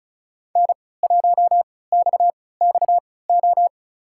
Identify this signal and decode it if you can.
Morse code, heard as a keyed tone, about 34 wpm, spaced as for 20 wpm, 700 Hz. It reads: N1XXO
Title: USA Call Signs - 5 Characters - 3x Character Spacing - 35wpm